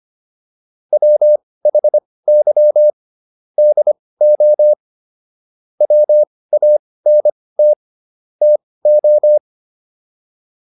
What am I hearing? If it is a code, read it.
Morse code, heard as a keyed tone, about 25 wpm, spaced as for 18 wpm, 600 Hz. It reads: WHY DO WANT TO